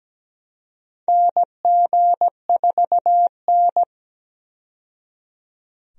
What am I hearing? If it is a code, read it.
Morse code, heard as a keyed tone, about 17 wpm, 700 Hz. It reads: NG4N
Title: USA Call Signs - 4 Characters - 17wpm